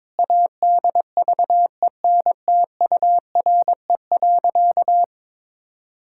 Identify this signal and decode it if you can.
Morse code, heard as a keyed tone, about 22 wpm, 700 Hz. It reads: ADVENTURE.
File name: practice / 22wpm / Sentences with 4-Word Phrases - 22wpm